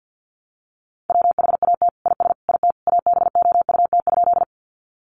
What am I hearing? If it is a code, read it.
Morse code, heard as a keyed tone, about 50 wpm, 700 Hz. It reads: WHAT IS IT ABOUT?
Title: Sentences from Top 100 Words 50wpm